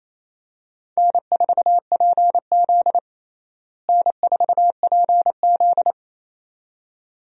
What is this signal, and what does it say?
Morse code, heard as a keyed tone, about 28 wpm, 700 Hz. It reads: N4PZ N4PZ
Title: USA Call Signs - 4 Characters - Repeated 2x - 28wpm